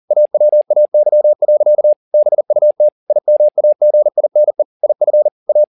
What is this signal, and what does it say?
Morse code, heard as a keyed tone, about 40 wpm, 600 Hz. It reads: AWAY. BUT IMAGINE IF A
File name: the old man and the sea 2 40wpm